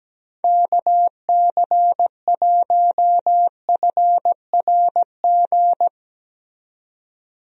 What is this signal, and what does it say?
Morse code, heard as a keyed tone, about 17 wpm, 700 Hz. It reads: KC1FRG
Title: USA Call Signs - 6 Characters - 17wpm